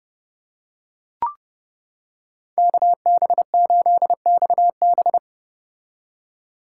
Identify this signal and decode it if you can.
Morse code, heard as a keyed tone, about 32 wpm, 700 Hz. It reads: KB8XB